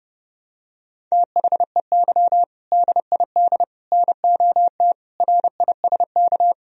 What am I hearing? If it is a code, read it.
Morse code, heard as a keyed tone, about 30 wpm, 700 Hz. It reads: THEY DID NOT RISK
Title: Sentences from Top 1000 Words - Review Part 4 - 30wpm